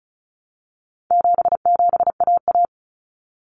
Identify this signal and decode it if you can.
Morse code, heard as a keyed tone, about 34 wpm, 700 Hz. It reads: 77AA
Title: Common Cut Numbers - 4-Digit 35wpm